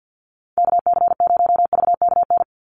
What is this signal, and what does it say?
Morse code, heard as a keyed tone, about 49 wpm, 700 Hz. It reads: KC0VKN
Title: Call Signs Repeated 3x - 50wpm